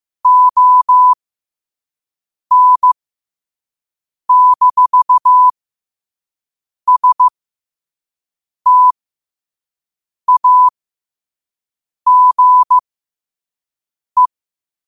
Morse code, heard as a keyed tone, about 15 wpm, spaced as for 5 wpm, 1000 Hz. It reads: ON-STAGE
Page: New quotes every day in morse code at 5 Words per minute.